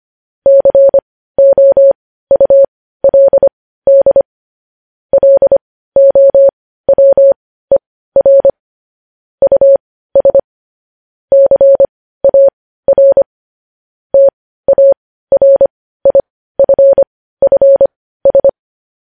Morse code, heard as a keyed tone, about 25 wpm, spaced as for 15 wpm, 550 Hz. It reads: COULD LOWER US CAR TARIFFS
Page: news headlines in morse code at 15 wpm